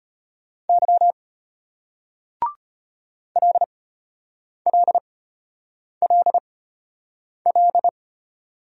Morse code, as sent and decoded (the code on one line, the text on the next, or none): -.-- / .-.. .-.. .-.. .-..
Y LLLL